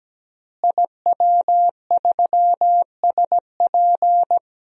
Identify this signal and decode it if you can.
Morse code, heard as a keyed tone, about 17 wpm, 700 Hz. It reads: IW3SP
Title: DXCC-100 Call Signs - 5 Characters - Repeated 2x - 17wpm